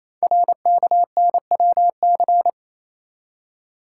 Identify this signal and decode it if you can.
Morse code, heard as a keyed tone, about 27 wpm, 700 Hz. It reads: RKNWC
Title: Code Groups - 5 Letters - 28wpm